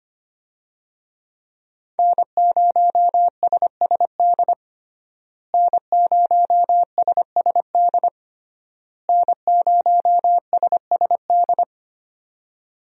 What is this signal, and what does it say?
Morse code, heard as a keyed tone, about 25 wpm, 700 Hz. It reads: N0SSD N0SSD N0SSD
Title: USA Call Signs - 5 Characters - Repeated 3x - 25wpm